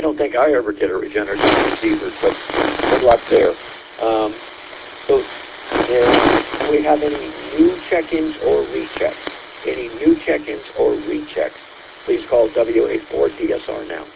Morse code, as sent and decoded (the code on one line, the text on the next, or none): none